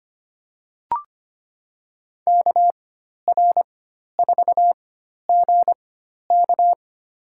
Morse code, heard as a keyed tone, about 25 wpm, spaced as for 12 wpm, 700 Hz. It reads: KR4GK
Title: USA Call Signs - 5 Characters - 4x Character Spacing - 25wpm